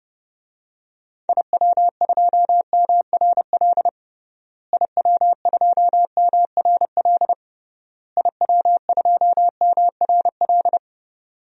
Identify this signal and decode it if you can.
Morse code, heard as a keyed tone, about 30 wpm, 700 Hz. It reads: IW2MRL IW2MRL IW2MRL
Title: DXCC-100 Call Signs - 6 Characters - Repeated 3x - 30wpm